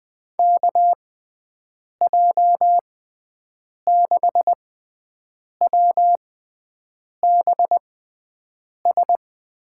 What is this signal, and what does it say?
Morse code, heard as a keyed tone, about 20 wpm, spaced as for 7 wpm, 700 Hz. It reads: KJ6WBS
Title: USA Call Signs - 6 Characters - 6x Character Spacing - 20wpm